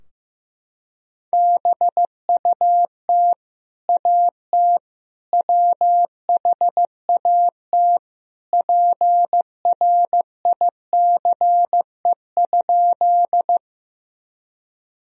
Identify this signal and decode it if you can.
Morse code, heard as a keyed tone, about 16 wpm, 700 Hz. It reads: BUT AT WHAT PRICE?
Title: Sentences from Top 1000 Words - Review Part 4 - 15wpm